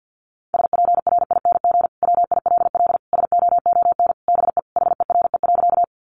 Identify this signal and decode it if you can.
Morse code, heard as a keyed tone, about 50 wpm, 700 Hz. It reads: SPRING WILL SOON BE HERE.
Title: Sentences from Top 1200 Words 50wpm